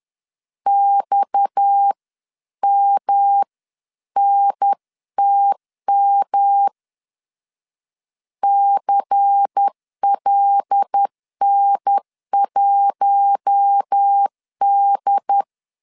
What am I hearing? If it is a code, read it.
Morse code, heard as a keyed tone, about 10 wpm, 800 Hz. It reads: XMNTM CLN1D